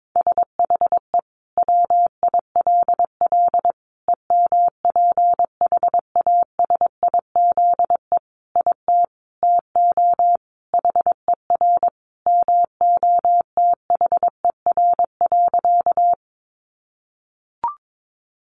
Morse code, heard as a keyed tone, about 22 wpm, 700 Hz. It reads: SHE WILL EMPHASIZE IT TO HER MOTHER.
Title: Sentences from Top 1500 Words 22wpm